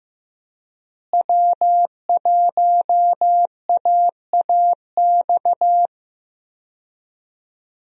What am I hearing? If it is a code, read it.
Morse code, heard as a keyed tone, about 16 wpm, 700 Hz. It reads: W1AAX
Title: USA Call Signs - 5 Characters - Repeated 2x - 15wpm